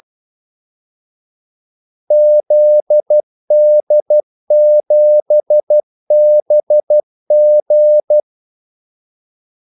Morse code, heard as a keyed tone, about 12 wpm, 600 Hz. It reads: ZD7BG
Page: 2 ekstra ordmellemrum mellem kaldesignalerne